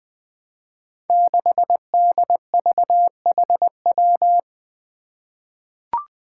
Morse code, heard as a keyed tone, about 21 wpm, 700 Hz. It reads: 6DVHW
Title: Five Challenging Letter-Number Combinations 20wpm